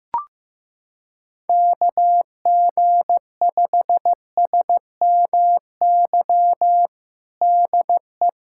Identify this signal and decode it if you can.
Morse code, heard as a keyed tone, about 15 wpm, 700 Hz. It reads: KG5SMY DE
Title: USA Callsign DE Callsign - 6 Characters - 15wpm